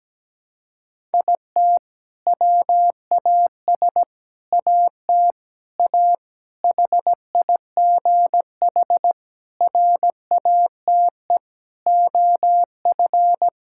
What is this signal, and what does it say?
Morse code, heard as a keyed tone, about 17 wpm, 700 Hz. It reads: IT WAS AT A HIGH RATE OF
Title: Sentences from Top 1000 Words - Review Part 5 - 17wpm